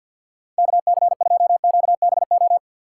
Morse code, heard as a keyed tone, about 53 wpm, 700 Hz. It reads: KC2QBO
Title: USA Call Signs - 6 Characters - 4x Character Spacing - 50wpm